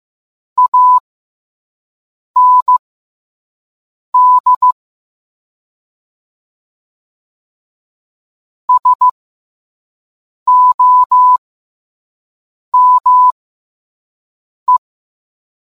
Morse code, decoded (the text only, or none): AND SOME